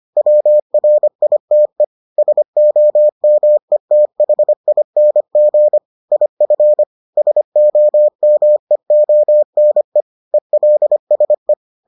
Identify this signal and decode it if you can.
Morse code, heard as a keyed tone, about 25 wpm, 600 Hz. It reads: WRITE SOMETHING IF SOMEONE ELSE